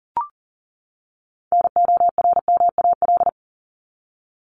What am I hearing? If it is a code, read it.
Morse code, heard as a keyed tone, about 40 wpm, 700 Hz. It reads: NORMAL
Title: Top 1001-1250 Words 40wpm